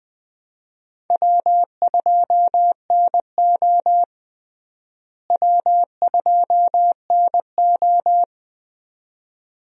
Morse code, heard as a keyed tone, about 21 wpm, 700 Hz. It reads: W2NO W2NO
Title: USA Call Signs - 4 Characters - Repeated 2x - 20wpm